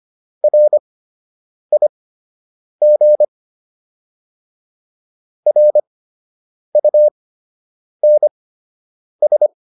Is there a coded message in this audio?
Yes: Morse code from 0.4 to 9.5 s, heard as a keyed tone, about 25 wpm, spaced as for 8 wpm, 600 Hz. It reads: RIG RUNS